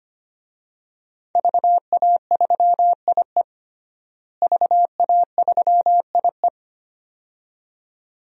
Morse code, heard as a keyed tone, about 25 wpm, 700 Hz. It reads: VA3IE VA3IE
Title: DXCC-100 Call Signs - 5 Characters - Repeated 2x - 25wpm